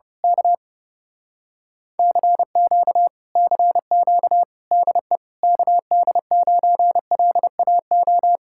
Morse code, heard as a keyed tone, about 29 wpm, 700 Hz. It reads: K CQ CQ DE KD9LAO
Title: Speed-Racing Common QSO Phrases - 30wpm